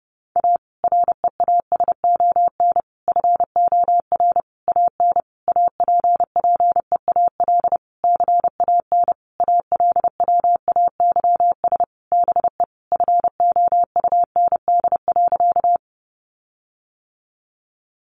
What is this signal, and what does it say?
Morse code, heard as a keyed tone, about 28 wpm, 700 Hz. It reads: A REASON FOR AN APPEAL CAN ALWAYS BE FOUND.
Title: Sentences from Top 1300 Words 30wpm